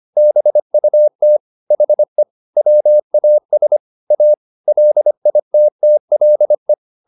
Morse code, heard as a keyed tone, about 25 wpm, 600 Hz. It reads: BUT HE WAS A LITTLE